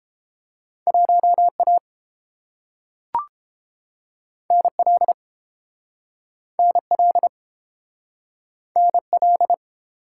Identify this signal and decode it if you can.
Morse code, heard as a keyed tone, about 30 wpm, 700 Hz. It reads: 1A NL NL NL